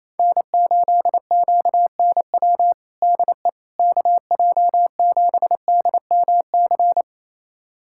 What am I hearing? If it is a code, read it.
Morse code, heard as a keyed tone, about 28 wpm, 700 Hz. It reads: N8QNW DE KJ7DMC
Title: USA Callsign DE Callsign 28wpm